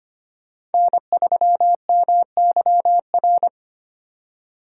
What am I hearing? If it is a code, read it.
Morse code, heard as a keyed tone, about 33 wpm, 700 Hz. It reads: N3MYR